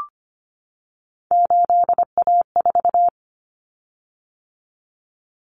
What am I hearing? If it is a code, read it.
Morse code, heard as a keyed tone, about 25 wpm, 700 Hz. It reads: 8A4
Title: Common Cut Numbers - 3-Digit 25wpm